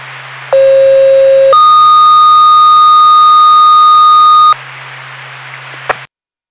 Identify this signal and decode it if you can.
Two-tone page, heard as a one-second tone, then a three-second tone, about 540 then 1230 Hz. It is heard from 0.5 to 4.5 s.